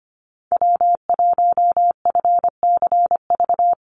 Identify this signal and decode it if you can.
Morse code, heard as a keyed tone, about 25 wpm, 700 Hz. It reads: W1FCV
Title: Call Signs - 25wpm